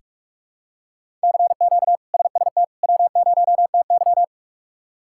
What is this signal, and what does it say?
Morse code, heard as a keyed tone, about 46 wpm, 700 Hz. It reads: CQ SST W0TY
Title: K1USN SST QSO Practice - 45wpm